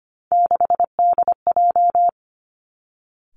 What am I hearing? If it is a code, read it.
Morse code, heard as a keyed tone, about 25 wpm, 700 Hz. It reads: 6DJ
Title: Three Challenging Letter-Number Combinations 25wpm